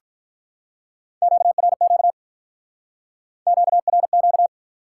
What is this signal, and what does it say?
Morse code, heard as a keyed tone, about 48 wpm, 700 Hz. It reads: QRQ QRQ